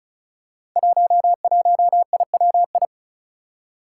Morse code, heard as a keyed tone, about 36 wpm, 700 Hz. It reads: 11IWI
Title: Five Challenging Letter-Number Combinations 35wpm